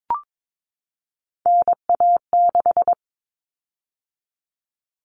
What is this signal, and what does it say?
Morse code, heard as a keyed tone, about 22 wpm, 700 Hz. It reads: NA6
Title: Common Cut Numbers - 3-Digit 22wpm